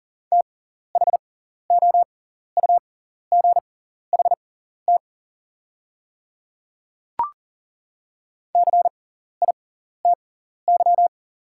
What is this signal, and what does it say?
Morse code, heard as a keyed tone, about 40 wpm, spaced as for 14 wpm, 700 Hz. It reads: THOUGHT CITY
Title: 6x Character Spacing - Top 300 Words - 40wpm